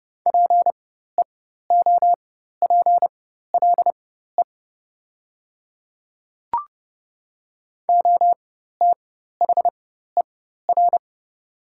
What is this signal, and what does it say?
Morse code, heard as a keyed tone, about 30 wpm, spaced as for 14 wpm, 700 Hz. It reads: PEOPLE OTHER
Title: 4x Character Spacing - Top 100 Words - 30wpm